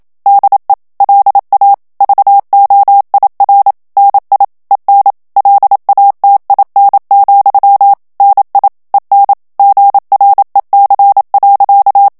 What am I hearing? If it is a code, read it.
Morse code, heard as a keyed tone, about 27 wpm, 800 Hz. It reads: DE LA VOIR NI EN LATIN, NI EN GREC.
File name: FRANCAIS TEXTE CLAIR 2 VITESSE 26 MOTS MINUTES